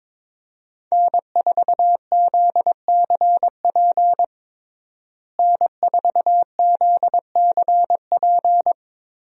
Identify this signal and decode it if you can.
Morse code, heard as a keyed tone, about 22 wpm, 700 Hz. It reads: N4ZCP N4ZCP
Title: USA Call Signs - 5 Characters - Repeated 2x - 22wpm